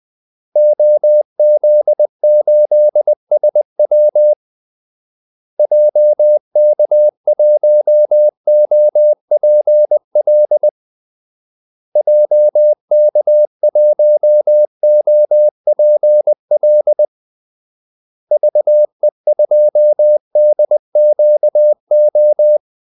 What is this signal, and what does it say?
Morse code, heard as a keyed tone, about 20 wpm, 600 Hz. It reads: OZ8SW JK1OPL JK1OPL VE2DQO
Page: Kaldesignaler 20 wpm